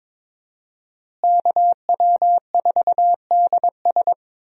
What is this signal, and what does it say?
Morse code, heard as a keyed tone, about 22 wpm, 700 Hz. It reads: KW4DS
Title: USA Call Signs - 5 Characters - Repeated 2x - 22wpm